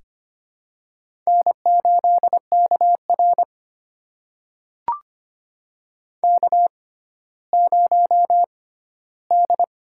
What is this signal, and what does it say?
Morse code, heard as a keyed tone, about 25 wpm, spaced as for 14 wpm, 700 Hz. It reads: N8KR K0D
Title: USA Call Signs - 4 Characters - 6x Character Spacing - 25wpm